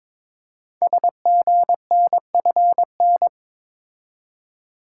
Morse code, heard as a keyed tone, about 21 wpm, 700 Hz. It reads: SGNFN
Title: Code Groups - 5 Letters - 22wpm